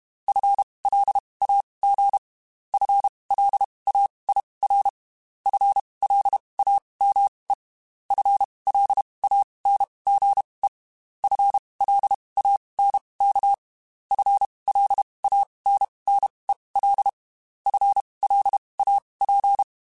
Morse code, heard as a keyed tone, about 32 wpm, spaced as for 23 wpm, 800 Hz. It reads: FLAG FLAIR FLAME FLANGE FLANK FLANNEL FLAP
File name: Spaced / FL Runs sp mp3s / FL Runs sp @30